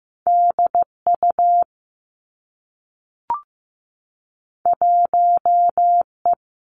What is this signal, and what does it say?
Morse code, heard as a keyed tone, about 15 wpm, 700 Hz. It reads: D U 1 E
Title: Two Challenging Letter-Number Combinations 15wpm